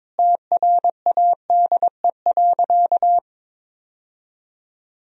Morse code, heard as a keyed tone, about 22 wpm, 700 Hz. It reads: TRADE.
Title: Sentences from Top 1000 Words - Review Part 4 - 22wpm